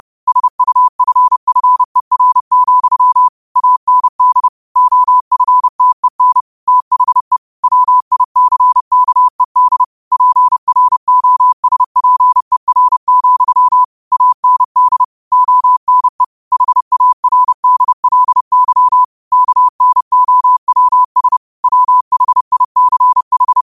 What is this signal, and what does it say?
Morse code, heard as a keyed tone, about 30 wpm, 1000 Hz. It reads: SUFFER, AND OFTEN THE WICKED PROSPER, AND ONE HARDLY KNOWS WHICH
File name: qotdfiles / 30wpm